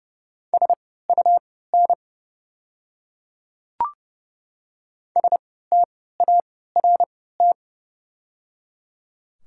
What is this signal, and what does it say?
Morse code, heard as a keyed tone, about 30 wpm, spaced as for 17 wpm, 700 Hz. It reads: SUN START
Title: 3x Character Spacing - Top 500 Words - 30wpm